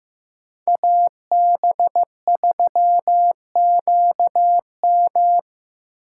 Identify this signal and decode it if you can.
Morse code, heard as a keyed tone, about 15 wpm, 700 Hz. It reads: AB3QM
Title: USA Call Signs - 8x Character Spacing - 15wpm